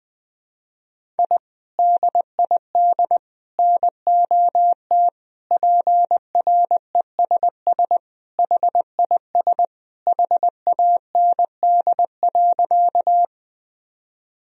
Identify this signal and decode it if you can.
Morse code, heard as a keyed tone, about 20 wpm, 700 Hz. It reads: I DID NOT PRESS HIS HAND.
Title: Sentences from Top 1000 Words - Review Part 1 - 20wpm